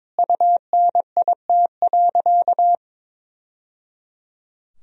Morse code, heard as a keyed tone, about 22 wpm, 700 Hz. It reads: UNIT.
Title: Sentences from Top 1000 Words - Review Part 2 - 22wpm